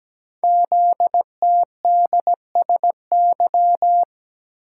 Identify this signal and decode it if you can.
Morse code, heard as a keyed tone, about 17 wpm, 700 Hz. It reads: ZTDSY
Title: Code Groups - 5 Letters - 17wpm